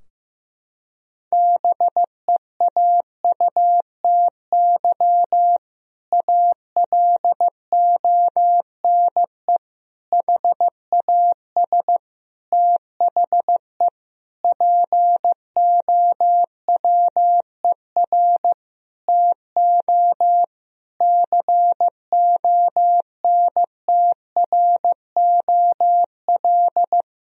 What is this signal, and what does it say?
Morse code, heard as a keyed tone, about 16 wpm, 700 Hz. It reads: BEAUTY ALONE HAS THE POWER TO CONTROL
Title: Sentences from Top 1000 Words - Review Part 2 - 15wpm